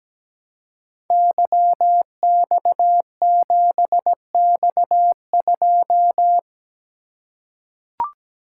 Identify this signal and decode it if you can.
Morse code, heard as a keyed tone, about 17 wpm, 700 Hz. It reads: YX7X2
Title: Code Groups - 5 Letter:Numbers - 17wpm